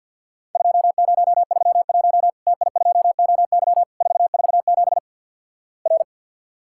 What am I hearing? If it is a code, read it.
Morse code, heard as a keyed tone, about 49 wpm, 700 Hz. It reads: 2031 TI2OY 447 R
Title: Worked All Europe DX Contest - QTC Exchange - 50wpm